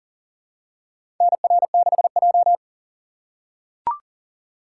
Morse code, heard as a keyed tone, about 40 wpm, 700 Hz. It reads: NR6J
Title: USA Call Signs - 4 Characters - 6x Character Spacing - 40wpm